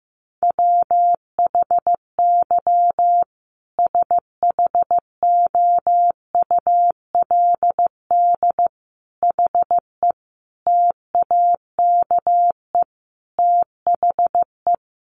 Morse code, heard as a keyed tone, about 15 wpm, 700 Hz. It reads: WHY SHOULD HE TAKE THE
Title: Sentences from Top 1500 Words 15wpm